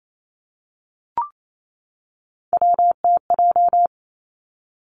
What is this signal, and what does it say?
Morse code, heard as a keyed tone, about 28 wpm, 700 Hz. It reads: WTJ
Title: Three Challenging Letter-Number Combinations 28wpm